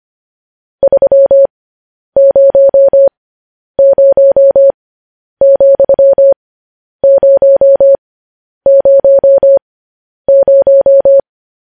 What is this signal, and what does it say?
Morse code, heard as a keyed tone, about 25 wpm, spaced as for 10 wpm, 550 Hz. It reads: 300,000
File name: podcast / News headlines-1x-25x10